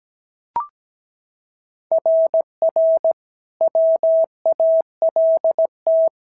Morse code, heard as a keyed tone, about 16 wpm, 650 Hz. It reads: RR WALT
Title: SOTA QSO Practice 17wpm